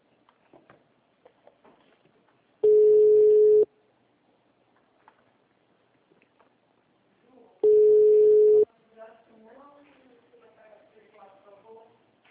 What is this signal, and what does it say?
Ringback tone (Europe, 425 Hz), with speech.
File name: out- -2014- - - .30